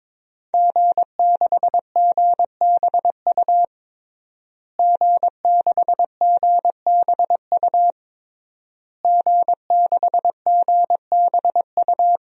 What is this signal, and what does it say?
Morse code, heard as a keyed tone, about 22 wpm, 700 Hz. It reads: G6GBU G6GBU G6GBU
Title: DXCC-100 Call Signs - 5 Characters - Repeated 3x - 22wpm